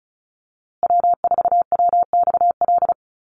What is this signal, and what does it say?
Morse code, heard as a keyed tone, about 34 wpm, 700 Hz. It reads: W4WXL
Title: USA Call Signs Repeated 2x 35wpm